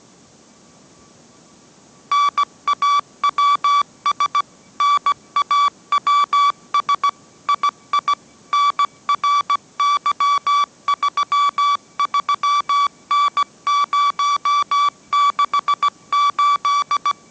Morse code, heard as a keyed tone, about 17 wpm, 1150 Hz. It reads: NAWS NAWS II NRY33N068
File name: mrx / mrx080 18wpm